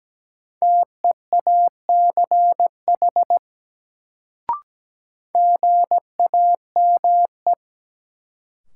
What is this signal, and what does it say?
Morse code, heard as a keyed tone, about 17 wpm, 700 Hz. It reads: TEACH GAME
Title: Sentences from Top 500 Words - New Word Drill - 17wpm